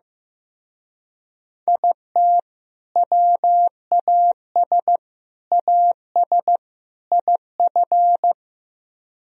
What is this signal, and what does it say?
Morse code, heard as a keyed tone, about 15 wpm, 700 Hz. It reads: IT WAS AS IF
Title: Speed-Racing - Sets of 4 Words 15wpm